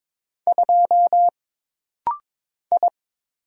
Morse code, heard as a keyed tone, about 22 wpm, 700 Hz. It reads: 2 I
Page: This rapid-fire timing is geared to push your ICR (Instant Character Recognition) skills.